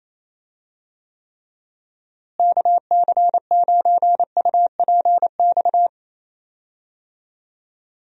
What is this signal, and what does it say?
Morse code, heard as a keyed tone, about 27 wpm, 700 Hz. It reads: KC9UPX
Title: USA Call Signs - 6 Characters - Repeated 2x - 28wpm